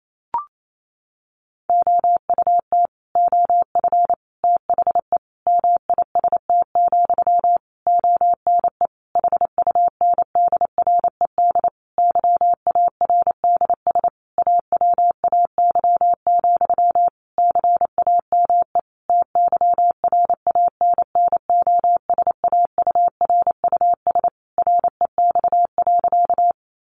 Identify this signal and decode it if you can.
Morse code, heard as a keyed tone, about 28 wpm, 700 Hz. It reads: OUT OF THE MIST, ONE HUNDRED YARDS AWAY, CAME TYRANNOSAURUS REX.
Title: A Sound Of Thunder 28wpm